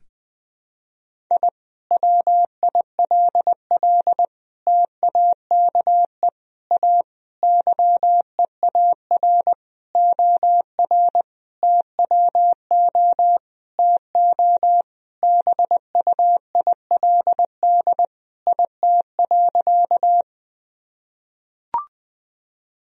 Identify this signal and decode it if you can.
Morse code, heard as a keyed tone, about 19 wpm, 700 Hz. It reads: I WILL TAKE A YEAR OR TWO TO BUILD IT.
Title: Sentences from Top 1000 Words - Review Part 1 - 20wpm